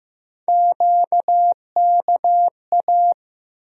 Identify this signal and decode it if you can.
Morse code, heard as a keyed tone, about 15 wpm, 700 Hz. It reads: QKA
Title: USA Call Sign Postfixes - 3 Letters 15wpm